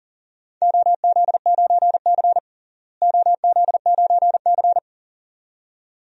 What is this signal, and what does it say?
Morse code, heard as a keyed tone, about 40 wpm, 700 Hz. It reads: OZ9C OZ9C
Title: DXCC-100 Call Signs - 4 Characters - Repeated 2x - 40wpm